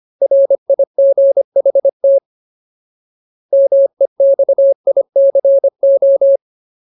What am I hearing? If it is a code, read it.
Morse code, heard as a keyed tone, about 25 wpm, 550 Hz. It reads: RIGHT MEXICO
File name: Ultimate Word List / 25WPM Ultimate